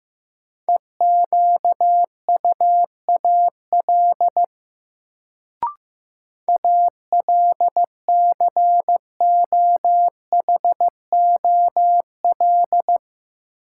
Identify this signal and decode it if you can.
Morse code, heard as a keyed tone, about 15 wpm, 700 Hz. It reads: EQUAL ALCOHOL
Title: Sentences from Top 1400 Words - New Word Drill - 15wpm